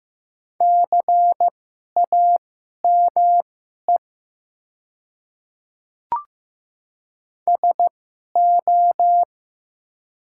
Morse code, heard as a keyed tone, about 15 wpm, spaced as for 11 wpm, 700 Hz. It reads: CAME SO
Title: 2x Character Spacing - Top 200 Words - 15wpm